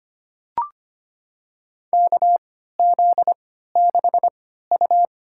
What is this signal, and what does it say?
Morse code, heard as a keyed tone, about 25 wpm, spaced as for 14 wpm, 700 Hz. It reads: KZ6U
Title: USA Call Signs - 3x Character Spacing - 25wpm